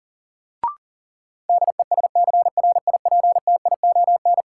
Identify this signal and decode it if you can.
Morse code, heard as a keyed tone, about 37 wpm, 700 Hz. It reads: DESCRIPTION
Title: Sentences from Top 1300 Words - New Word Drill - 40wpm